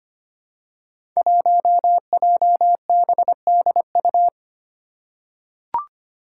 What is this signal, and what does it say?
Morse code, heard as a keyed tone, about 25 wpm, 700 Hz. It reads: 1JBDU
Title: Five Challenging Letter-Number Combinations 25wpm